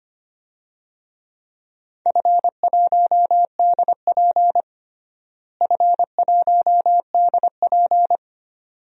Morse code, heard as a keyed tone, about 25 wpm, 700 Hz. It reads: F1DP F1DP
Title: DXCC-100 Call Signs - 4 Characters - Repeated 2x - 25wpm